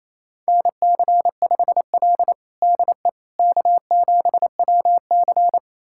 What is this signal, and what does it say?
Morse code, heard as a keyed tone, about 27 wpm, 700 Hz. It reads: NC5L DE K7WC
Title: USA Callsign DE Callsign - 4 Characters - 28wpm